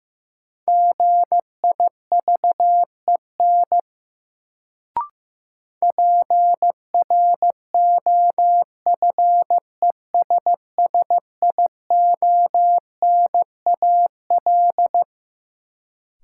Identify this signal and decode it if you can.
Morse code, heard as a keyed tone, about 15 wpm, 700 Hz. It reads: GIVEN PROFESSIONAL
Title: Sentences from Top 1400 Words - New Word Drill Alt - 15wpm